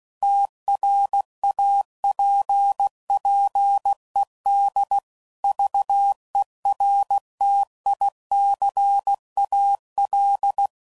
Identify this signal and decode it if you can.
Morse code, heard as a keyed tone, about 16 wpm, 800 Hz. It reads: TRAPPED VERTICAL
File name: CW Code Groups @15WPM / QSOs / kc7haa